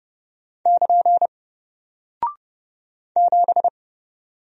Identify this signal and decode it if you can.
Morse code, heard as a keyed tone, about 30 wpm, 700 Hz. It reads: <KN> 7
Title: Single Character (Letter:Number:Punctuation:Prosign) 30wpm